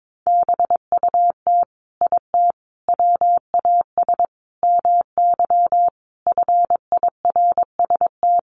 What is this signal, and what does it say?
Morse code, heard as a keyed tone, about 22 wpm, 700 Hz. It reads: BUT IT WAS MY FIRST
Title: Sentences from Top 1300 Words 22wpm